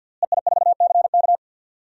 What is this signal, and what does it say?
Morse code, heard as a keyed tone, about 47 wpm, 700 Hz. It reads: EI4CK
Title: DXCC-100 Call Signs - 5 Characters - Repeated 3x - 50wpm